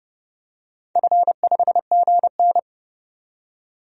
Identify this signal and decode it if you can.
Morse code, heard as a keyed tone, about 29 wpm, 700 Hz. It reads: F5GN
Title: DXCC-100 Call Signs - 4 Characters - 30wpm